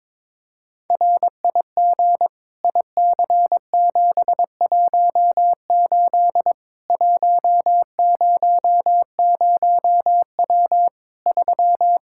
Morse code, heard as a keyed tone, about 22 wpm, 700 Hz. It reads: RIG IC718 100W 3
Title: Practice QSO - Exchange 2 - Info - 22wpm